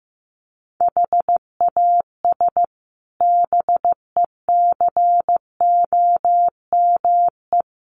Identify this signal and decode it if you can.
Morse code, heard as a keyed tone, about 15 wpm, 700 Hz. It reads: HAS BECOME
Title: Sets of 5 Words Encore 15wpm